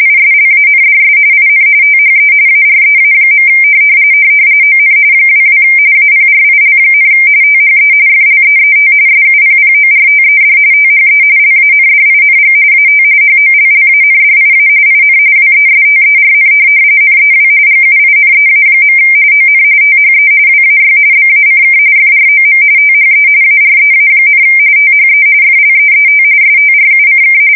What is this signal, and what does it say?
RTTY, about 45.45 baud. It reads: ISLANDS IN THE AREA. ON 22 APRIL, THE FLEET SUPPORTED THE LANDING AT HOLLANDIA IN WESTERN NEW GUINEA. ON THE WAY BACK FROM THE AREA, THE FLEET STRUCK
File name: fsk170 / USS-Massachusetts